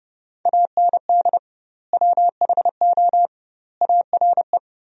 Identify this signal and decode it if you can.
Morse code, heard as a keyed tone, about 30 wpm, 700 Hz. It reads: AND WHO ARE
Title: 2x Word-Spacing - Sentences from Top 100 Words - 30wpm